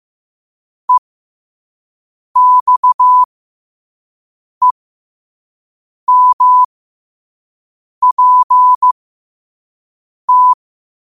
Morse code, heard as a keyed tone, about 15 wpm, spaced as for 5 wpm, 1000 Hz. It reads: EXEMPT
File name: qotdfiles / 5wpm